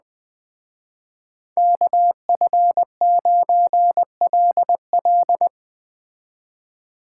Morse code, heard as a keyed tone, about 21 wpm, 700 Hz. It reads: KF9LL
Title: Speed Racing - USA 2x2 Call Signs 17wpm